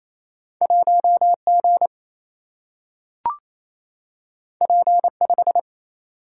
Morse code, heard as a keyed tone, about 28 wpm, 700 Hz. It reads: 1G P5
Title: Single-Letter Single-Number 28wpm